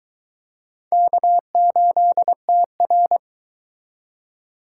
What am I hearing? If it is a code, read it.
Morse code, heard as a keyed tone, about 22 wpm, 700 Hz. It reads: K8TR